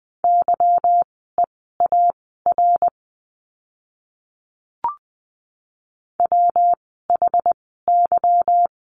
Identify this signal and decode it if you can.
Morse code, heard as a keyed tone, about 20 wpm, spaced as for 15 wpm, 700 Hz. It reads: YEAR WHY
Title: 230 Lesson 37f - Letter x - Words 20wpm